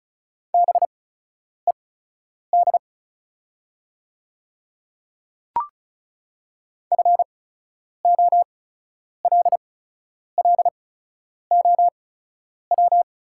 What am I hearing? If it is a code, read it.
Morse code, heard as a keyed tone, about 36 wpm, spaced as for 10 wpm, 700 Hz. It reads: BED FOLLOW
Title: 8x Character Spacing - Top 500 Words - 35wpm